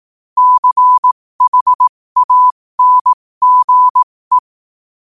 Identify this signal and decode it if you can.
Morse code, heard as a keyed tone, about 18 wpm, spaced as for 15 wpm, 1000 Hz. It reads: CHANGE
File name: qotdfiles / 15wpm